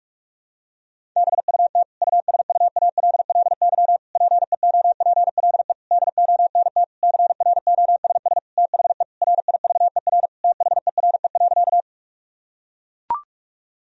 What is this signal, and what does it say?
Morse code, heard as a keyed tone, about 41 wpm, 700 Hz. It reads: BUT USUALLY PEOPLE DONT CROSS THE RIVER THERE.
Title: Sentences from Top 1000 Words - Review Part 3 - 45wpm